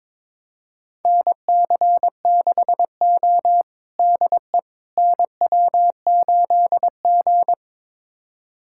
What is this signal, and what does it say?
Morse code, heard as a keyed tone, about 22 wpm, 700 Hz. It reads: NC6O DE NW8G
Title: USA Callsign DE Callsign - 4 Characters - 22wpm